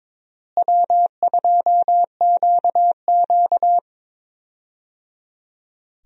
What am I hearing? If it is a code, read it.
Morse code, heard as a keyed tone, about 22 wpm, 700 Hz. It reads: W2QQ
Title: USA Call Signs - 4 Characters - Repeated 3x - 22wpm